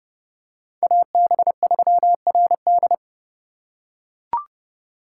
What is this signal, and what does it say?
Morse code, heard as a keyed tone, about 30 wpm, 700 Hz. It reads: AB3RD